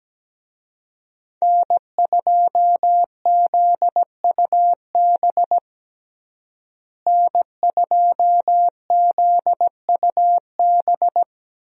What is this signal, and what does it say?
Morse code, heard as a keyed tone, about 17 wpm, 700 Hz. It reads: N2ZUB N2ZUB
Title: USA Call Signs - 5 Characters - Repeated 3x - 17wpm